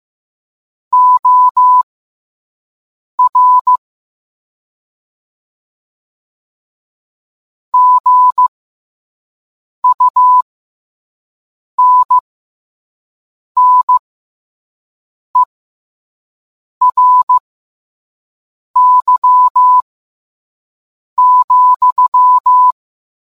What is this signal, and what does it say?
Morse code, heard as a keyed tone, about 15 wpm, spaced as for 5 wpm, 1000 Hz. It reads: OR GUNNERY,